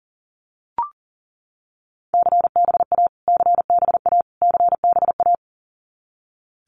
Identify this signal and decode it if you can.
Morse code, heard as a keyed tone, about 40 wpm, 700 Hz. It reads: CBA CBA CBA
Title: Abbreviations Repeated 3x - 40wpm